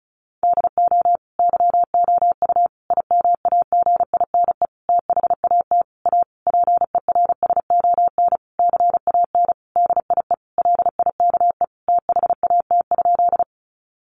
Morse code, heard as a keyed tone, about 34 wpm, 700 Hz. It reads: DO YOU IMAGINE THAT A PERSON CAN DIE LIKE THAT?
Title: Sentences from Top 1400 Words 35wpm